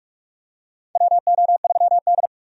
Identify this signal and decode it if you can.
Morse code, heard as a keyed tone, about 42 wpm, 700 Hz. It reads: WO3D
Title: USA Call Signs - 4 Characters - 45wpm